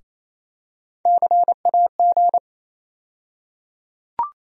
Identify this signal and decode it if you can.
Morse code, heard as a keyed tone, about 28 wpm, 700 Hz. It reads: CAG
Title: USA Call Sign Postfixes - 3 Letters 28wpm